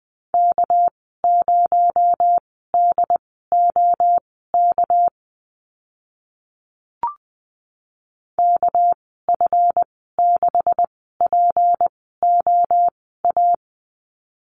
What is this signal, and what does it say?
Morse code, heard as a keyed tone, about 20 wpm, spaced as for 15 wpm, 700 Hz. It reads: K0DOK KF6POA
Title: 238 Lesson 38g - Letter q - Call Signs 20wpm